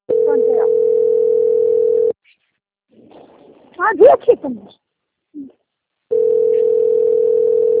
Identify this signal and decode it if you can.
Ringback tone (North America), 440 + 480 Hz, with speech.
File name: out- - - - - .14916